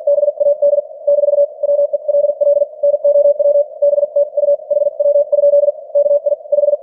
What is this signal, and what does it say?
Morse code, heard as a keyed tone, about 50 wpm, spaced as for 43 wpm, 580 Hz. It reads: BAD - WELL NOW BTU HW? KI 5
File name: archived / CWBros